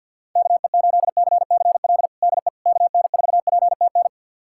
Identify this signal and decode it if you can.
Morse code, heard as a keyed tone, about 46 wpm, 700 Hz. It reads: KE8CCL DE KN4PTN
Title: USA Callsign DE Callsign - 6 Characters - 50wpm